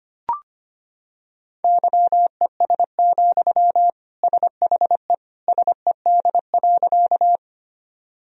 Morse code, heard as a keyed tone, about 25 wpm, 700 Hz. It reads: YES, SHE SED.
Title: Sentences from Top 500 Words and Abbreviations 25wpm